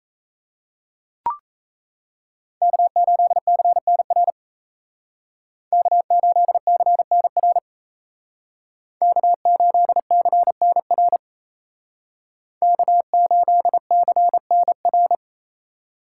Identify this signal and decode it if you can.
Morse code, heard as a keyed tone, about 34 wpm, 700 Hz. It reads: K8CNR K8CNR K8CNR K8CNR
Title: Speed Racing - USA 1x3 Call Signs 28wpm